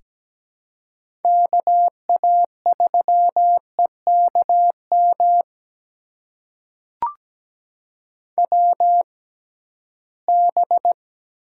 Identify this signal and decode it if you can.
Morse code, heard as a keyed tone, about 17 wpm, spaced as for 13 wpm, 700 Hz. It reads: KA3EKM WB
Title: USA Call Signs - 6 Characters - 6x Character Spacing - 17wpm